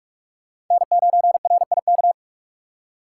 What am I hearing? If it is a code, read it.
Morse code, heard as a keyed tone, about 43 wpm, 700 Hz. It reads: N9RIK
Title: USA Call Signs - 6x Character Spacing - 45wpm